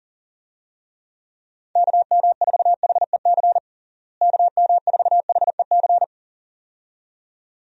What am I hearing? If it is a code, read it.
Morse code, heard as a keyed tone, about 37 wpm, 700 Hz. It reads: KM4HEC KM4HEC
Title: USA Call Signs - 6 Characters - Repeated 2x - 40wpm